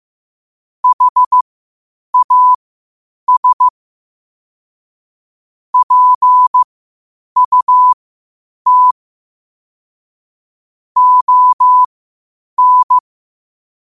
Morse code, heard as a keyed tone, about 15 wpm, spaced as for 8 wpm, 1000 Hz. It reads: HAS PUT ON
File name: qotdfiles / 8wpm